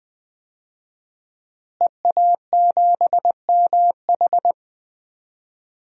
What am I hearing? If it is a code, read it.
Morse code, heard as a keyed tone, about 20 wpm, 700 Hz. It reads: EA7MH
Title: DXCC-100 Call Signs - 5 Characters - Repeated 2x - 20wpm